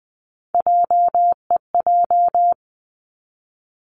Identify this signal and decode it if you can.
Morse code, heard as a keyed tone, about 20 wpm, 700 Hz. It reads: JEJ
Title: Three Challenging Letter-Number Combinations 20wpm